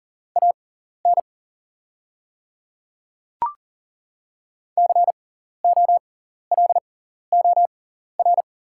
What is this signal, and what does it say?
Morse code, heard as a keyed tone, about 40 wpm, spaced as for 14 wpm, 700 Hz. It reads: AN COLOR
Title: 6x Character Spacing - Top 300 Words - 40wpm